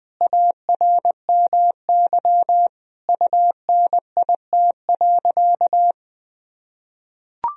Morse code, heard as a keyed tone, about 20 wpm, 700 Hz. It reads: ARMY UNIT.
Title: Sentences from Top 1000 Words - Review Part 2 - 20wpm